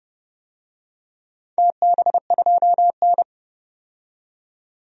Morse code, heard as a keyed tone, about 30 wpm, 700 Hz. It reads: TB2N
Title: DXCC-100 Call Signs - 4 Characters - 30wpm